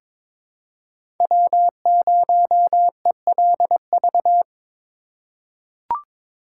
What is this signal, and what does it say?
Morse code, heard as a keyed tone, about 22 wpm, 700 Hz. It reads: W0ELV
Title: Code Groups - 5 Letter:Numbers - 22wpm